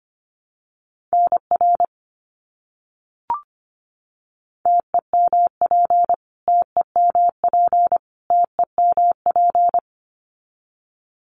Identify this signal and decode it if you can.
Morse code, heard as a keyed tone, about 25 wpm, 700 Hz. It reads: NR TEMP TEMP TEMP
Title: Abbreviations Repeated 3x - 25wpm